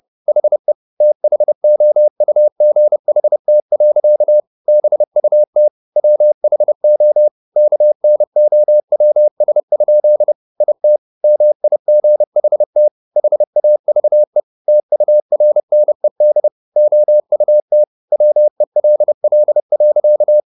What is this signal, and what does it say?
Morse code, heard as a keyed tone, about 30 wpm, 600 Hz. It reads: HE THOUGHT. BUT WHO KNOWS? IT MIGHT HAVE TURNED OUT WELL.